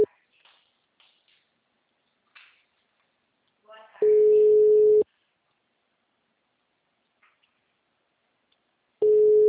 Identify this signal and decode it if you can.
Ringback tone (Europe, 425 Hz), with speech.
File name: out- -2002- - - .57163